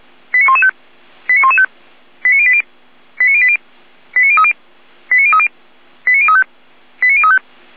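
Five-tone selective call (ZVEI), heard as quick runs of five tones, tones of about 70 ms. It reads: 89115 89115 89080 89080 89030 89030 89035 89035
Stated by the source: ZVEI five-tone call received by SDR.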